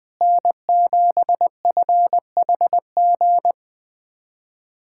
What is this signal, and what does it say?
Morse code, heard as a keyed tone, about 20 wpm, 700 Hz. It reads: N7FHG
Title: USA Callsign DE Callsign - 5 Characters - 20wpm